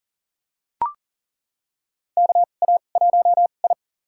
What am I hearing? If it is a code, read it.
Morse code, heard as a keyed tone, about 40 wpm, spaced as for 29 wpm, 700 Hz. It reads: KA1I
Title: USA Call Signs - 2x Character Spacing - 40wpm